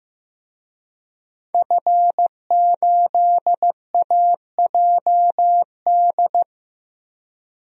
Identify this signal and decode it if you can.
Morse code, heard as a keyed tone, about 15 wpm, 700 Hz. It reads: F8AJD
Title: DXCC-100 Call Signs - 5 Characters - 15wpm